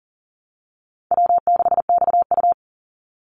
Morse code, heard as a keyed tone, about 40 wpm, 700 Hz. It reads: W6XU
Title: Easy Call Signs Repeated 2x - 40wpm